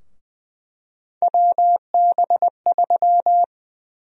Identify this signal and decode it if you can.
Morse code, heard as a keyed tone, about 20 wpm, 700 Hz. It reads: WB3
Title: USA Call Sign Prefixes - 20wpm